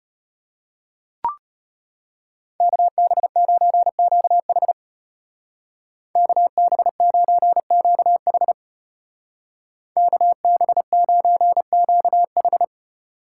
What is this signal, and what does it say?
Morse code, heard as a keyed tone, about 32 wpm, 700 Hz. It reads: KB9QH KB9QH KB9QH